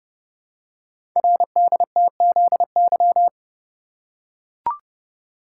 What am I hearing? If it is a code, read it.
Morse code, heard as a keyed tone, about 30 wpm, 700 Hz. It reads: RDTZY
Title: Code Groups - 5 Letters - 30wpm